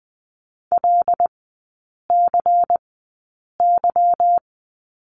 Morse code, heard as a keyed tone, about 20 wpm, spaced as for 8 wpm, 690 Hz. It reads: LCY
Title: 198 Lesson 32e - Letter k - Three Characters 20wpm